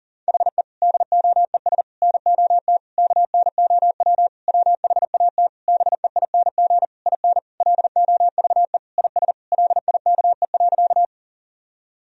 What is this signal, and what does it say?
Morse code, heard as a keyed tone, about 40 wpm, 700 Hz. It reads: HE DOES NOT KNOW WHAT BEING IN LOVE IS LIKE.
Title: Sentences from Top 1000 Words - Review Part 3 - 40wpm